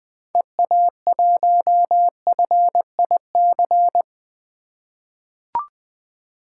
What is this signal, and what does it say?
Morse code, heard as a keyed tone, about 21 wpm, 700 Hz. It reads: EA1FIC
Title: 2023 POTA Hunter Call Signs - 20wpm